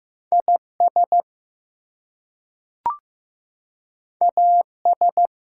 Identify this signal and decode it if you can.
Morse code, heard as a keyed tone, about 15 wpm, 700 Hz. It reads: IS AS
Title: Two Letter Words - 15wpm